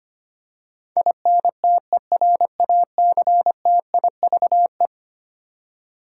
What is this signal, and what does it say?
Morse code, heard as a keyed tone, about 25 wpm, 700 Hz. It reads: INTERACTIVE
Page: morse-code-ninja-practice-25wpm